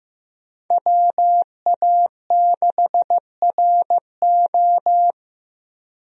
Morse code, heard as a keyed tone, about 15 wpm, 700 Hz. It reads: WA6RO
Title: USA Call Signs - 5 Characters - 15wpm